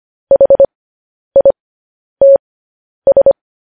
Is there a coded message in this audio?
Yes: Morse code from 0.3 to 3.3 s, heard as a keyed tone, about 25 wpm, spaced as for 10 wpm, 550 Hz. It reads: HITS